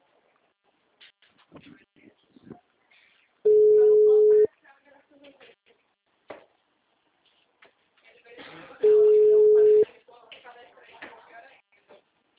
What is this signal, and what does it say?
Ringback tone (Europe, 425 Hz), with speech.
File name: out- -2041- - - .59575